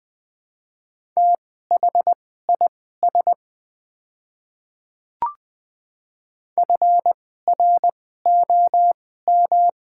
Morse code, heard as a keyed tone, about 20 wpm, spaced as for 14 wpm, 700 Hz. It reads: THIS FROM
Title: 2x Character Spacing - Top 200 Words - 20wpm